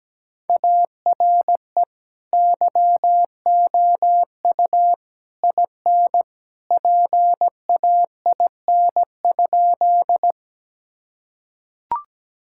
Morse code, heard as a keyed tone, about 17 wpm, 700 Hz. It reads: ARE YOU IN PAIN?
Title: Sentences from Top 1600 Words 17wpm